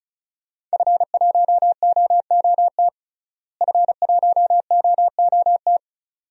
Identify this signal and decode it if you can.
Morse code, heard as a keyed tone, about 36 wpm, 700 Hz. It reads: F1OOT F1OOT
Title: DXCC-100 Call Signs - 5 Characters - Repeated 2x - 35wpm